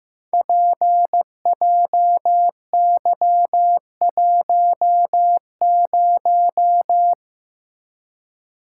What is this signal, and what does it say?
Morse code, heard as a keyed tone, about 15 wpm, 700 Hz. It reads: PJY10
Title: Code Groups - 5 Letter:Numbers - 15wpm